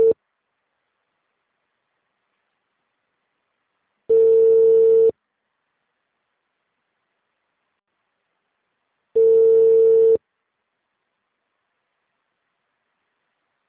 Ringback tone (China, 450 Hz).